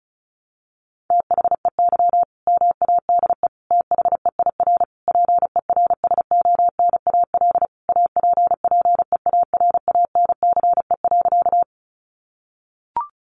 Morse code, heard as a keyed tone, about 34 wpm, 700 Hz. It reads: THEY MADE THEIR PERSONAL APPEARANCE.
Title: Sentences from Top 1100 Words 35wpm